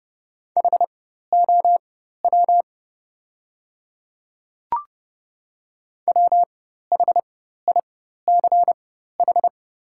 Morse code, heard as a keyed tone, about 30 wpm, spaced as for 14 wpm, 700 Hz. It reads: HOW WHICH
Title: 4x Character Spacing - Top 100 Words - 30wpm